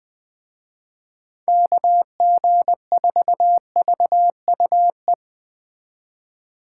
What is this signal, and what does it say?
Morse code, heard as a keyed tone, about 21 wpm, 700 Hz. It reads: KG4VUE
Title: Speed Racing - USA 2x3 Call Signs 17wpm